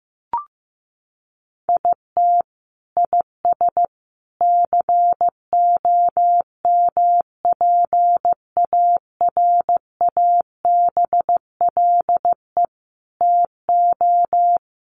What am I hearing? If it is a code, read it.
Morse code, heard as a keyed tone, about 15 wpm, 700 Hz. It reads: IT IS COMPARABLE TO
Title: A Sound Of Thunder 15wpm